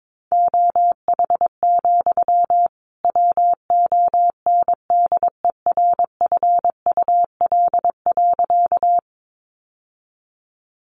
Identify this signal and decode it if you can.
Morse code, heard as a keyed tone, about 22 wpm, 700 Hz. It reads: OH, WONDERFUL.
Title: Sentences from Top 1100 Words 22wpm